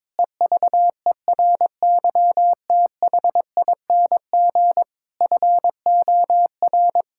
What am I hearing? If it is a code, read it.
Morse code, heard as a keyed tone, about 22 wpm, 700 Hz. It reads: EVERYTHING FOR
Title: Sentences from Top 1000 Words - Review Part 6 - 22wpm